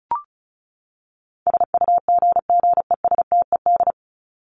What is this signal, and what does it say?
Morse code, heard as a keyed tone, about 34 wpm, 700 Hz. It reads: SUGGESTED
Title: ED Suffix - Top 100 Words with Base Words 35wpm